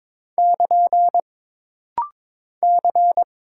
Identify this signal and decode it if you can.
Morse code, heard as a keyed tone, about 22 wpm, 700 Hz. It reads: <KN> C
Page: This rapid-fire timing is geared to push your ICR (Instant Character Recognition) skills.